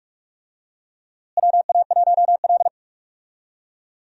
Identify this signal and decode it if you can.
Morse code, heard as a keyed tone, about 46 wpm, 700 Hz. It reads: WA1L
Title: USA Call Signs - 4 Characters - Repeated 2x - 45wpm